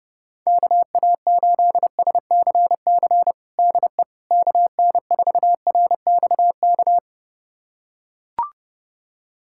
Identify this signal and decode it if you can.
Morse code, heard as a keyed tone, about 30 wpm, 700 Hz. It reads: KA8SCC DE KN4RXK
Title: USA Callsign DE Callsign 30wpm